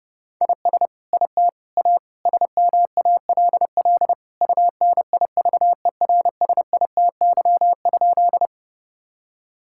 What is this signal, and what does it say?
Morse code, heard as a keyed tone, about 30 wpm, 700 Hz. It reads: IS IT A SMALL UNIVERSITY?
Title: Sentences from Top 1000 Words - Review Part 5 - 30wpm